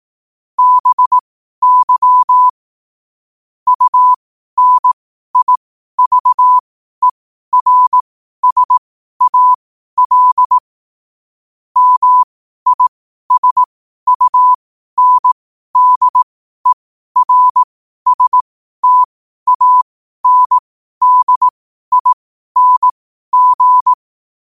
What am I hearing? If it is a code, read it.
Morse code, heard as a keyed tone, about 18 wpm, spaced as for 12 wpm, 1000 Hz. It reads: BY UNIVERSAL MISUNDERSTANDING
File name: qotdfiles / 12wpm